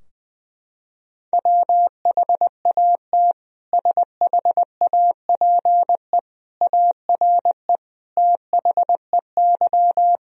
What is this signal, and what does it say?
Morse code, heard as a keyed tone, about 20 wpm, 700 Hz. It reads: WHAT SHAPE ARE THEY
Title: Sentences from Top 1000 Words - Review Part 3 - 20wpm